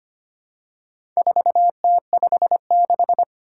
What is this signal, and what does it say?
Morse code, heard as a keyed tone, about 25 wpm, 700 Hz. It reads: 4T56
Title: Four Challenging Letter-Number Combinations 25wpm